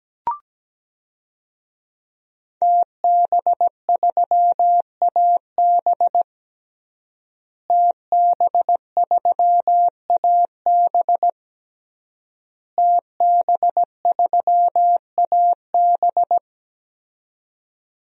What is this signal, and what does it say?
Morse code, heard as a keyed tone, about 17 wpm, 700 Hz. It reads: TB3AB TB3AB TB3AB
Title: DXCC-100 Call Signs - 5 Characters - Repeated 3x - 17wpm